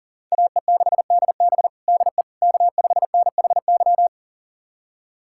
Morse code, heard as a keyed tone, about 40 wpm, 700 Hz. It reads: AE6DB DE K5NHY
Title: USA Callsign DE Callsign - 5 Characters - 40wpm